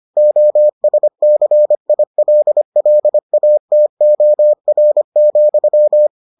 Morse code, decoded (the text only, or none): OSCILLATOR,